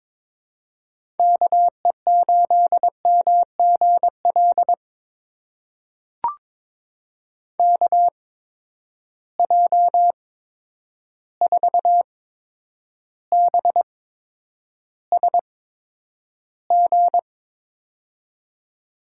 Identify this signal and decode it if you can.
Morse code, heard as a keyed tone, about 22 wpm, spaced as for 9 wpm, 700 Hz. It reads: KE8MGL KJ4BSG
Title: USA Call Signs - 6 Characters - 8x Character Spacing - 22wpm